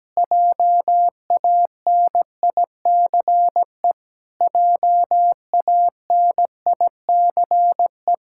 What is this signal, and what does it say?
Morse code, heard as a keyed tone, about 17 wpm, 700 Hz. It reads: JANICE JANICE
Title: Practice QSO - Exchange 1 - Beginning - 17wpm